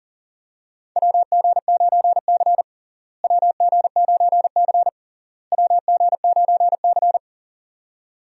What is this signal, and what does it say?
Morse code, heard as a keyed tone, about 40 wpm, 700 Hz. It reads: WG9C WG9C WG9C
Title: USA Call Signs - 4 Characters - Repeated 3x - 40wpm